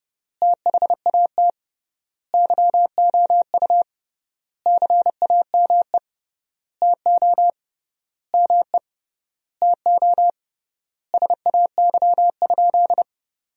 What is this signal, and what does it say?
Morse code, heard as a keyed tone, about 30 wpm, 700 Hz. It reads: THAT YOU CAME TO ME TO SAY?
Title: 3x Word-Spacing - Sentences from Top 200 Words - 30wpm